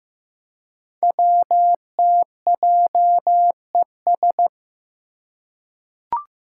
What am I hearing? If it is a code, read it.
Morse code, heard as a keyed tone, about 15 wpm, 700 Hz. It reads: WTJES
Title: Code Groups - 5 Letters - 15wpm